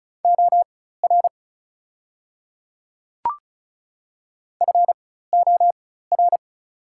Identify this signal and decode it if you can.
Morse code, heard as a keyed tone, about 34 wpm, spaced as for 16 wpm, 700 Hz. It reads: OR FOR
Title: 4x Character Spacing - Top 500 Words - 35wpm